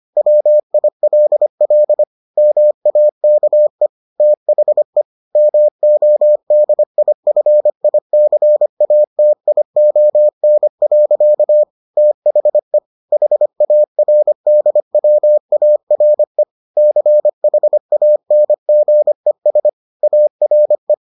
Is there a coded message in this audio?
Yes: Morse code from 0.2 to 20.9 s, heard as a keyed tone, about 25 wpm, 600 Hz. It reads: WILL MAKE THE MODIFICATION. THE HARDWARE CHANGES ARE